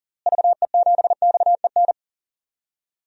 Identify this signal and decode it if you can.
Morse code, heard as a keyed tone, about 40 wpm, 700 Hz. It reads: VE7XEN
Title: DXCC-100 Call Signs - 6 Characters - Repeated 2x - 40wpm